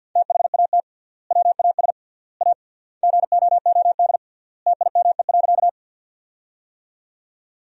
Morse code, heard as a keyed tone, about 53 wpm, 700 Hz. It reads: THAT WAS A GOOD TIME.
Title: 3x Word-Spacing - Sentences from Top 200 Words - 50wpm